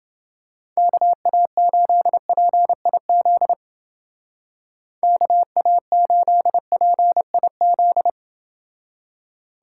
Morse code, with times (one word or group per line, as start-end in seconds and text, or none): 0.8-3.5 KA8PIZ
5.0-8.1 KA8PIZ